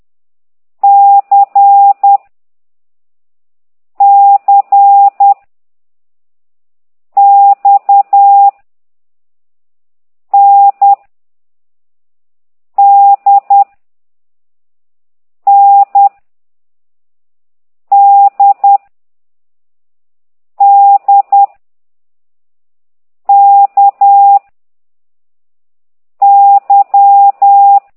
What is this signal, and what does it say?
Morse code, heard as a keyed tone, about 10 wpm, 800 Hz. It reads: C C X N D N D D K Y
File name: ct2idw / exerc4